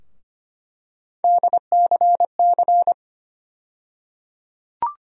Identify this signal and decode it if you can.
Morse code, heard as a keyed tone, about 25 wpm, 700 Hz. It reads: DCC
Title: USA Call Sign Postfixes - 3 Letters 25wpm